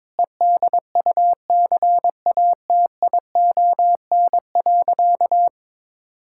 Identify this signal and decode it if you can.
Morse code, heard as a keyed tone, about 22 wpm, 700 Hz. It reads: EDUCATION.
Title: Sentences from Top 1000 Words - Review Part 6 - 22wpm